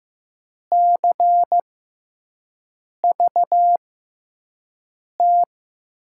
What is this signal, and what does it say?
Morse code, heard as a keyed tone, about 15 wpm, spaced as for 5 wpm, 700 Hz. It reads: CVT
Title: USA Call Signs - 6 Characters - 6x Character Spacing - 15wpm